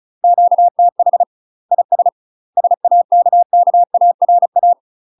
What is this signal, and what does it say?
Morse code, heard as a keyed tone, about 34 wpm, 700 Hz. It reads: QTH IS SAKKARA
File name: UK MASTER QSOs TRAINING / 35WPM / UK MASTER QSOs 0003